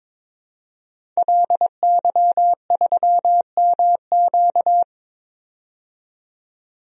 Morse code, heard as a keyed tone, about 22 wpm, 700 Hz. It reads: LY3MQ
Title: DXCC-100 Call Signs - 5 Characters - 22wpm